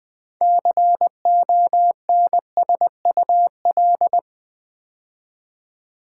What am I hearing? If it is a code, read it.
Morse code, heard as a keyed tone, about 20 wpm, 700 Hz. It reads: CONSUL
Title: Build A Word - 6 Letters At A Time - 12-Letter Words - 20wpm